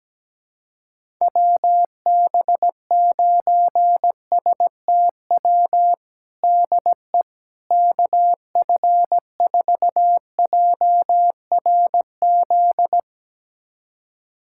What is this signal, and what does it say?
Morse code, heard as a keyed tone, about 17 wpm, 700 Hz. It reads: WB9STW DE KF4JRZ
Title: USA Callsign DE Callsign 17wpm